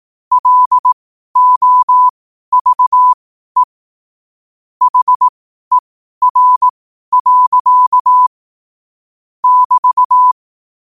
Morse code, heard as a keyed tone, about 18 wpm, spaced as for 12 wpm, 1000 Hz. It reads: LOVE HER. <BT>
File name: qotdfiles / 12wpm